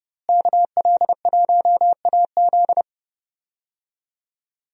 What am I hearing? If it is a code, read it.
Morse code, heard as a keyed tone, about 30 wpm, 700 Hz. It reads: KL1AZ
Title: USA Call Signs - 5 Characters - Repeated 3x - 30wpm